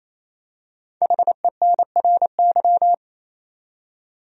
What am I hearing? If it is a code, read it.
Morse code, heard as a keyed tone, about 27 wpm, 700 Hz. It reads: HENRY
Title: USA Common Names - 28wpm